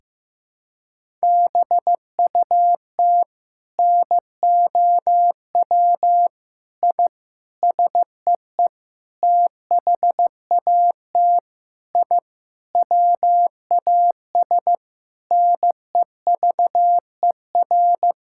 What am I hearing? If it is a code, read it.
Morse code, heard as a keyed tone, about 15 wpm, 700 Hz. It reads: BUT NOW I SEE THAT I WAS NEVER
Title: Sentences from Top 1000 Words - Review Part 6 - 15wpm